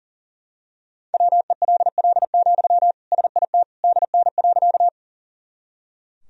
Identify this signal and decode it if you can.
Morse code, heard as a keyed tone, about 37 wpm, 700 Hz. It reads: WELL, SIT DN.
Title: Sentences from Top 500 Words and Abbreviations 40wpm